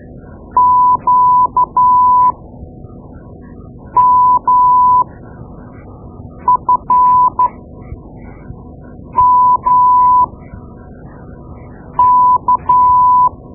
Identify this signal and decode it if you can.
Morse code, heard as a keyed tone, about 13 wpm, spaced as for 4 wpm, 1000 Hz. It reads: QMFMK
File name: Morse Lessons / Morse Lessons / Tape-2B